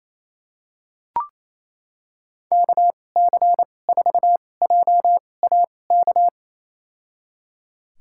Morse code, heard as a keyed tone, about 28 wpm, spaced as for 20 wpm, 700 Hz. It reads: KC4JAK
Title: USA Call Signs - 6 Characters - 2x Character Spacing - 28wpm